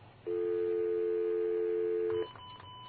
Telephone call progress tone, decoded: dial tone